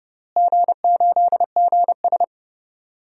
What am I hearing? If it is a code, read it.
Morse code, heard as a keyed tone, about 31 wpm, 700 Hz. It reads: G8GS
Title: DXCC-100 Call Signs - 4 Characters - Repeated 3x - 30wpm